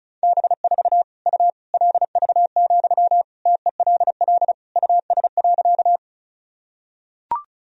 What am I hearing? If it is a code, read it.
Morse code, heard as a keyed tone, about 36 wpm, 700 Hz. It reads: B4 U LV, TELL US.
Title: Sentences from Top 400 Words and Abbreviations 35wpm